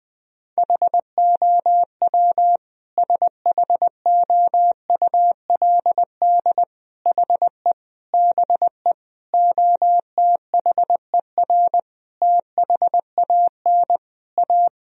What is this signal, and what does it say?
Morse code, heard as a keyed tone, about 20 wpm, 700 Hz. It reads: HOW SHOULD HE BE OTHER THAN A
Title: Sentences from Top 1000 Words - Review Part 2 - 20wpm